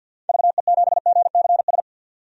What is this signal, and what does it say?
Morse code, heard as a keyed tone, about 48 wpm, 700 Hz. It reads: VE7GCS
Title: DXCC-100 Call Signs - 6 Characters - 50wpm